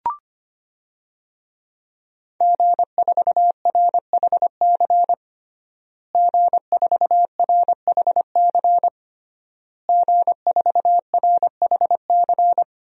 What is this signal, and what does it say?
Morse code, heard as a keyed tone, about 25 wpm, 700 Hz. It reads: G4RHC G4RHC G4RHC
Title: DXCC-100 Call Signs - 5 Characters - Repeated 3x - 25wpm